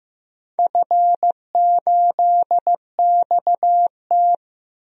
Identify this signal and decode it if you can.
Morse code, heard as a keyed tone, about 15 wpm, 700 Hz. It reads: F8XT
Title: DXCC-100 Call Signs - 4 Characters - 15wpm